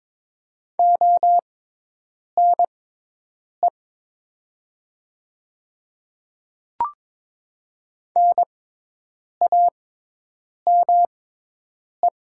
Morse code, heard as a keyed tone, about 22 wpm, spaced as for 8 wpm, 700 Hz. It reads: ONE NAME
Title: 6x Character Spacing - Top 200 Words - 22wpm